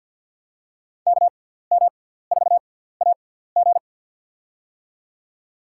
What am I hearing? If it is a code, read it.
Morse code, heard as a keyed tone, about 51 wpm, spaced as for 17 wpm, 700 Hz. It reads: KM4AG
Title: USA Call Signs - 5 Characters - 6x Character Spacing - 50wpm